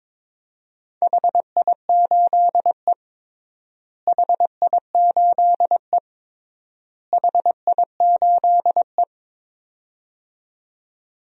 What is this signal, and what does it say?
Morse code, heard as a keyed tone, about 21 wpm, 700 Hz. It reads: HI8E HI8E HI8E
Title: DXCC-100 Call Signs - 4 Characters - Repeated 3x - 22wpm